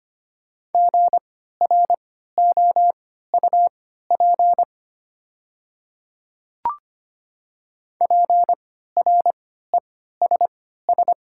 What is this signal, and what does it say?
Morse code, heard as a keyed tone, about 25 wpm, spaced as for 14 wpm, 700 Hz. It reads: GROUP PRESS
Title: 3x Character Spacing - Top 300 Words - 25wpm